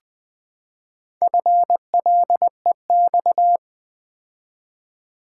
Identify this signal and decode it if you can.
Morse code, heard as a keyed tone, about 20 wpm, 700 Hz. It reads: FLEX
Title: Build A Word - 4 Letters At A Time - 11-Letter Words - 20wpm